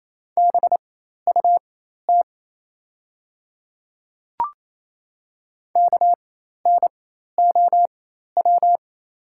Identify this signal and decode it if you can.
Morse code, heard as a keyed tone, about 28 wpm, spaced as for 13 wpm, 700 Hz. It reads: BUT KNOW
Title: 4x Character Spacing - Top 500 Words - 28wpm